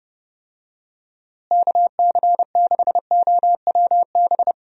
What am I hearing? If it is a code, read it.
Morse code, heard as a keyed tone, about 30 wpm, 700 Hz. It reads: KC6OWB
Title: USA Call Signs - 6 Characters - Repeated 3x - 30wpm